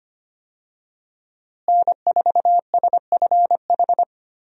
Morse code, heard as a keyed tone, about 25 wpm, 700 Hz. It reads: N4SFH
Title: USA Call Signs - 5 Characters - Repeated 2x - 25wpm